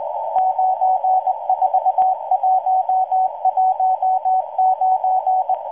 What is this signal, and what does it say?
Morse code, heard as a keyed tone, about 19 wpm, 750 Hz. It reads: 94119